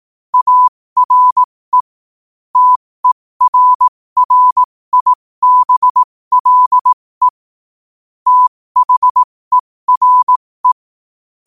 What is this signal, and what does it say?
Morse code, heard as a keyed tone, about 18 wpm, spaced as for 15 wpm, 1000 Hz. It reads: ARE TERRIBLE THERE
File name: qotdfiles / 15wpm